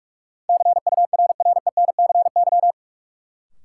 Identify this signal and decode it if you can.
Morse code, heard as a keyed tone, about 46 wpm, 700 Hz. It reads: CURRENCY
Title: Sentences from Top 1500 Words - New Word Drill - 45wpm